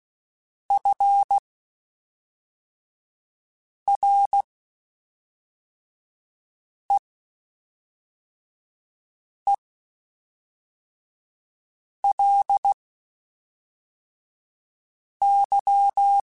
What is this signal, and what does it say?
Morse code, heard as a keyed tone, about 16 wpm, spaced as for 3 wpm, 800 Hz. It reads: FREELY
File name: Quotes Spaced / Quotes Spaced-mp3s / Advice sp @5